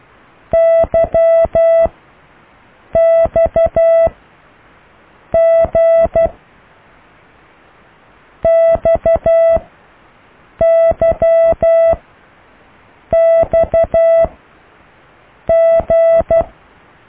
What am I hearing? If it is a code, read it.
Morse code, heard as a keyed tone, about 12 wpm, 660 Hz. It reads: Y X G X Y X G